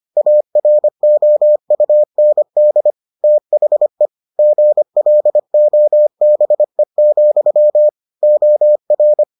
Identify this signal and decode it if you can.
Morse code, heard as a keyed tone, about 25 wpm, 600 Hz. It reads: AROUND THE GLOBE, OR